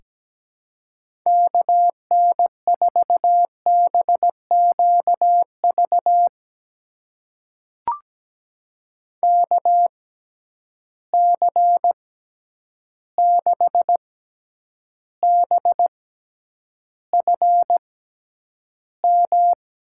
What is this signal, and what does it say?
Morse code, heard as a keyed tone, about 17 wpm, spaced as for 9 wpm, 700 Hz. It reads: KN4BQV KC6BFM
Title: USA Call Signs - 6 Characters - 6x Character Spacing - 17wpm